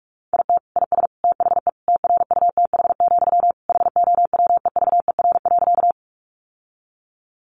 Morse code, heard as a keyed tone, about 44 wpm, 700 Hz. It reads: IT IS THE TRUTH, HOWEVER.
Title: Sentences from Top 800 Words 45wpm